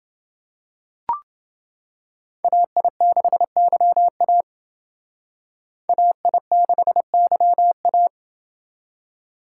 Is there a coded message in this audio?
Yes: Morse code from 2.4 to 8.1 s, heard as a keyed tone, about 28 wpm, 700 Hz. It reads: AI6YA AI6YA